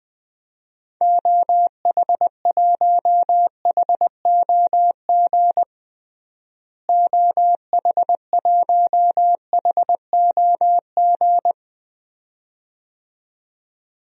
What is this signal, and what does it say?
Morse code, heard as a keyed tone, about 21 wpm, 700 Hz. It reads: OH1HOG OH1HOG
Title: DXCC-100 Call Signs - 6 Characters - Repeated 2x - 20wpm